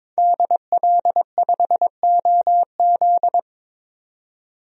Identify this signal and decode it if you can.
Morse code, heard as a keyed tone, about 21 wpm, 700 Hz. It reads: DL5OZ
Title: DXCC-100 Call Signs - 5 Characters - 22wpm